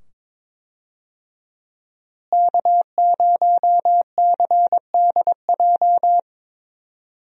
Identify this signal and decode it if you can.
Morse code, heard as a keyed tone, about 22 wpm, 700 Hz. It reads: K0CDJ
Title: USA Call Signs - 5 Characters - Repeated 2x - 22wpm